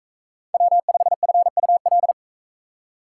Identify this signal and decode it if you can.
Morse code, heard as a keyed tone, about 41 wpm, 700 Hz. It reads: W5FUL